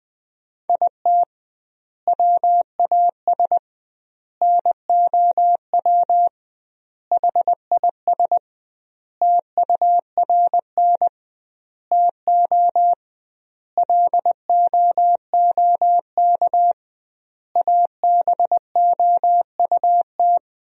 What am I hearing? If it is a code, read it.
Morse code, heard as a keyed tone, about 21 wpm, 700 Hz. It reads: IT WAS NOW HIS TURN TO LOOK ABOUT
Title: 2x Word-Spacing - Sentences from Top 200 Words - 20wpm